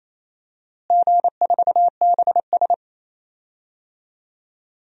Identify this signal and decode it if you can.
Morse code, heard as a keyed tone, about 28 wpm, 700 Hz. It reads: G4BS
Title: DXCC-100 Call Signs - 4 Characters - Repeated 2x - 28wpm